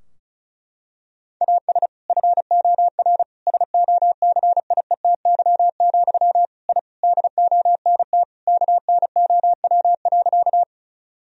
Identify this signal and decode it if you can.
Morse code, heard as a keyed tone, about 34 wpm, 700 Hz. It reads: AS FOR SOCIETY, I DONT KNOW.
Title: Sentences from Top 1000 Words - Review Part 4 - 35wpm